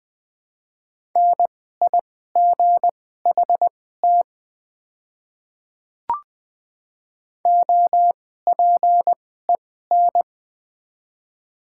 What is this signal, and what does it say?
Morse code, heard as a keyed tone, about 21 wpm, spaced as for 15 wpm, 700 Hz. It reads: NIGHT OPEN
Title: 2x Character Spacing - Top 300 Words - 20wpm